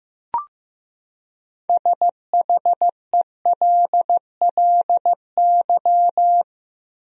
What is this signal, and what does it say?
Morse code, heard as a keyed tone, about 15 wpm, 700 Hz. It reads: SHELLY
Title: USA Common Names - 15wpm